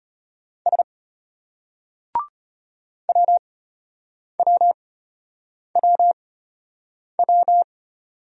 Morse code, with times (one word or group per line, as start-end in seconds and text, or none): 0.7-0.8 S
3.1-7.6 WWWW